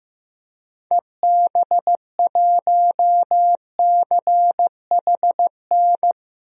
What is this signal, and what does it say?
Morse code, heard as a keyed tone, about 15 wpm, 700 Hz. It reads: EB1CHN
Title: DXCC-100 Call Signs - 6 Characters - 15wpm